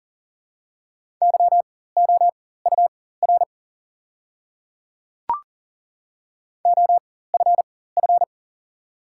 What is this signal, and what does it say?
Morse code, heard as a keyed tone, about 40 wpm, spaced as for 19 wpm, 700 Hz. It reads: YOUR OFF
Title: 4x Character Spacing - Top 200 Words - 40wpm